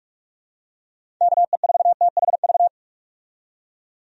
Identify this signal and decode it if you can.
Morse code, heard as a keyed tone, about 46 wpm, 700 Hz. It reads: KE4THV
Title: USA Call Signs - 6 Characters - 45wpm